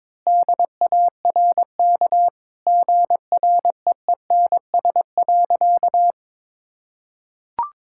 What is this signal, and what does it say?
Morse code, heard as a keyed tone, about 22 wpm, 700 Hz. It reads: DARK GREENS.
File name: practice / 22wpm / Easy Sentences with Hints - 22wpm